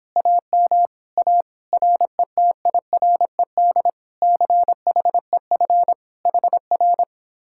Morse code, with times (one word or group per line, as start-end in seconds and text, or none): 0.2-0.9 AM
1.2-1.4 A
1.7-3.9 RETIRED
4.2-5.9 CHEF
6.3-7.0 HR